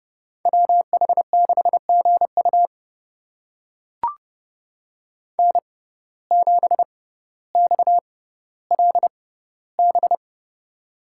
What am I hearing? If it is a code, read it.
Morse code, heard as a keyed tone, about 30 wpm, spaced as for 15 wpm, 700 Hz. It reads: WH6GU N7XLB
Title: USA Call Signs - 5 Characters - 6x Character Spacing - 30wpm